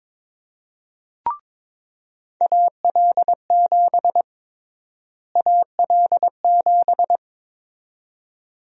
Morse code, heard as a keyed tone, about 22 wpm, 700 Hz. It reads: AL7 AL7
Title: USA Call Sign Prefixes - Repeated 2x - 22wpm